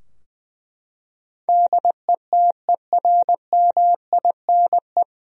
Morse code, heard as a keyed tone, about 20 wpm, 700 Hz. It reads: DETERMINE
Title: Sentences from Top 1000 Words - Review Part 4 - 20wpm